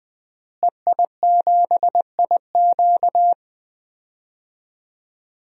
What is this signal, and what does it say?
Morse code, heard as a keyed tone, about 20 wpm, 700 Hz. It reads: EI7IQ
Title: DXCC-100 Call Signs - 5 Characters - Repeated 3x - 20wpm